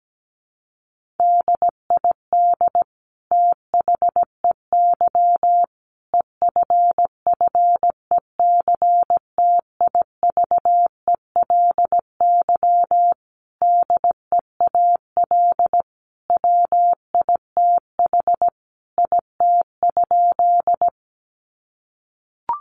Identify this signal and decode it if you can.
Morse code, heard as a keyed tone, about 17 wpm, 700 Hz. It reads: DID THEY EFFECTIVELY DEAL WITH IT?
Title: Sentences from Top 1400 Words 17wpm